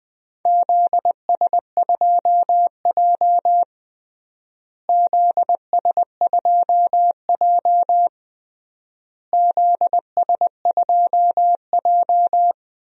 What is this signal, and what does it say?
Morse code, heard as a keyed tone, about 20 wpm, 700 Hz. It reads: ZS2J ZS2J ZS2J
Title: DXCC-100 Call Signs - 4 Characters - Repeated 3x - 20wpm